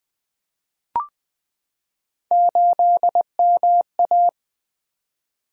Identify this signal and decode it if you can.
Morse code, heard as a keyed tone, about 20 wpm, 700 Hz. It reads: 8MA
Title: Three Letter-Number Combinations - 20wpm